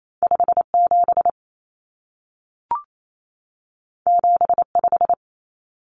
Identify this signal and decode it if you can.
Morse code, heard as a keyed tone, about 28 wpm, 700 Hz. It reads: 57 75
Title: Common Cut Numbers - 2-Digit 28wpm